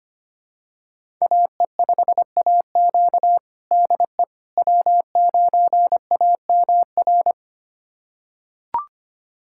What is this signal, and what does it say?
Morse code, heard as a keyed tone, about 25 wpm, 700 Hz. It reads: AE5AQ DE W9AMR
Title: USA Callsign DE Callsign - 5 Characters - 25wpm